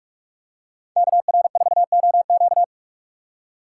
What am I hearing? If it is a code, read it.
Morse code, heard as a keyed tone, about 44 wpm, 700 Hz. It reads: KR4OQ